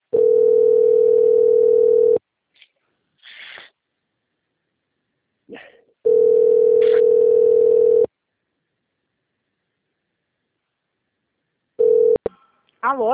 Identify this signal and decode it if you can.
Ringback tone (North America), 440 + 480 Hz, with speech.
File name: out- - - - - .19556